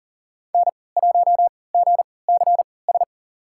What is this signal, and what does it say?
Morse code, heard as a keyed tone, about 40 wpm, spaced as for 23 wpm, 700 Hz. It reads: N1GCS
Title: USA Call Signs - 5 Characters - 3x Character Spacing - 40wpm